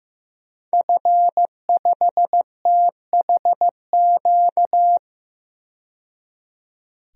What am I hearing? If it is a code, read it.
Morse code, heard as a keyed tone, about 15 wpm, 700 Hz. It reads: F5THQ
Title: DXCC-100 Call Signs - 5 Characters - 15wpm